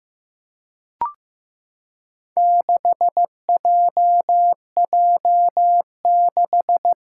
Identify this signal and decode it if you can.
Morse code, heard as a keyed tone, about 15 wpm, 700 Hz. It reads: 6JJ6
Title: Four Challenging Letter-Number Combinations 15wpm